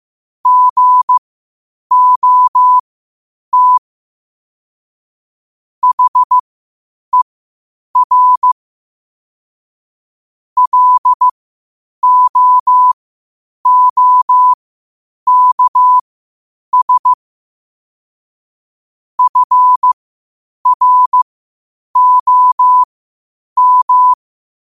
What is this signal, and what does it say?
Morse code, heard as a keyed tone, about 15 wpm, spaced as for 8 wpm, 1000 Hz. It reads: GOT HER LOOKS FROM
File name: qotdfiles / 8wpm